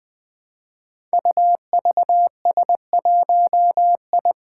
Morse code, heard as a keyed tone, about 20 wpm, 700 Hz. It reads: UVS1I
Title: Five Challenging Letter-Number Combinations 20wpm